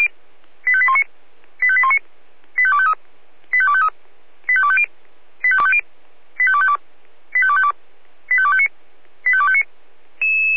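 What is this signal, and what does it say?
Five-tone selective call (ZVEI), heard as quick runs of five tones, tones of about 70 ms. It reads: 85819 85819 85253 85253 85250 85250 85252 85252 85259 85259
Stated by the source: ZVEI five-tone selective call received on an SDR.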